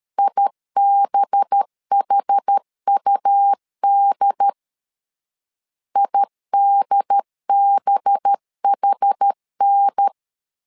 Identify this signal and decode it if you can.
Morse code, heard as a keyed tone, about 13 wpm, 800 Hz. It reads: IBHUD IDBHN
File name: LEZIONI / 5 LBVC / 5 LB 12wpm 1906